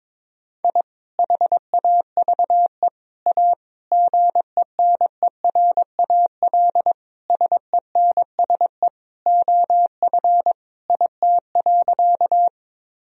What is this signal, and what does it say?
Morse code, heard as a keyed tone, about 22 wpm, 700 Hz. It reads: I HAVE A GENERAL SENSE OF IT.
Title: Sentences from Top 1000 Words - Review Part 4 - 22wpm